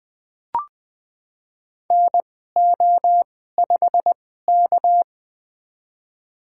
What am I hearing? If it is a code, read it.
Morse code, heard as a keyed tone, about 20 wpm, spaced as for 14 wpm, 700 Hz. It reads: NO5K
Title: USA Call Signs - 4 Characters - 2x Character Spacing - 20wpm